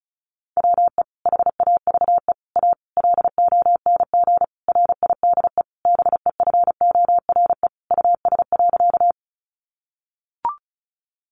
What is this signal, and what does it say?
Morse code, heard as a keyed tone, about 34 wpm, 700 Hz. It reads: WE HAVE A LONG RIDE BEFORE US.
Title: Sentences from Top 1200 Words 35wpm